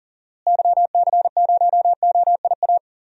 Morse code, heard as a keyed tone, about 40 wpm, 700 Hz. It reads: YC0OIA
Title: DXCC-100 Call Signs - 6 Characters - Repeated 2x - 40wpm